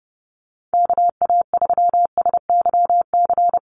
Morse code, heard as a keyed tone, about 30 wpm, 700 Hz. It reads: KA3SYC
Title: USA Call Signs Repeated 2x 30wpm